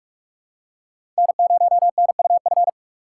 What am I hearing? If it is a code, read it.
Morse code, heard as a keyed tone, about 46 wpm, 700 Hz. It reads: N0NUF
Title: USA Call Signs - 5 Characters - 4x Character Spacing - 45wpm